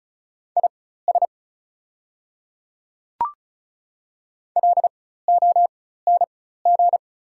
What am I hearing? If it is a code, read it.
Morse code, heard as a keyed tone, about 34 wpm, spaced as for 16 wpm, 700 Hz. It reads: IS LONG
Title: 4x Character Spacing - Top 500 Words - 35wpm